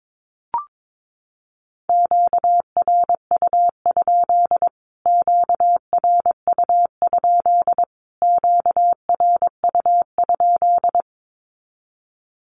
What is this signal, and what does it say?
Morse code, heard as a keyed tone, about 22 wpm, 700 Hz. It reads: QRU? QRU? QRU?
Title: Q-Codes Repeated 3x 22wpm